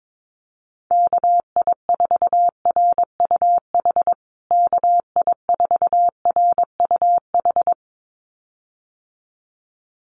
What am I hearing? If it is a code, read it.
Morse code, heard as a keyed tone, about 22 wpm, 700 Hz. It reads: KI4RUH KI4RUH
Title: USA Call Signs Repeated 2x 22wpm